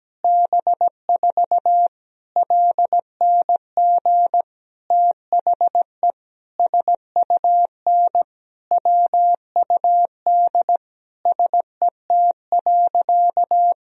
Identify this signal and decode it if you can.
Morse code, heard as a keyed tone, about 17 wpm, 700 Hz. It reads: B4 LNG THE SUN WUD SET.
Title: Sentences from Top 400 Words and Abbreviations 17wpm